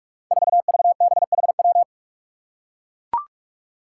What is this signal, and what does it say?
Morse code, heard as a keyed tone, about 44 wpm, 700 Hz. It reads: 4VBHW
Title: Five Challenging Letter-Number Combinations 45wpm